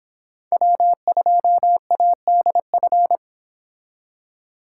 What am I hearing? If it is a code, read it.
Morse code, heard as a keyed tone, about 25 wpm, 700 Hz. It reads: W2ADF